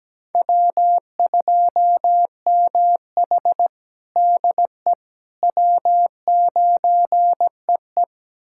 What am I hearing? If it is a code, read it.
Morse code, heard as a keyed tone, about 17 wpm, 700 Hz. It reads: W2MH DE W9EE
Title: USA Callsign DE Callsign - 4 Characters - 17wpm